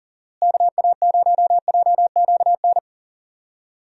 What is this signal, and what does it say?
Morse code, heard as a keyed tone, about 38 wpm, 700 Hz. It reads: KA0JQN
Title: USA Call Signs - 6 Characters - Repeated 3x - 40wpm